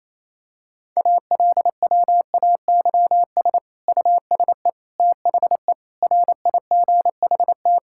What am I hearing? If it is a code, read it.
Morse code, heard as a keyed tone, about 28 wpm, 700 Hz. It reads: ALWAYS USE THE RIGHT
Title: Sentences from Top 1000 Words - Review Part 6 - 28wpm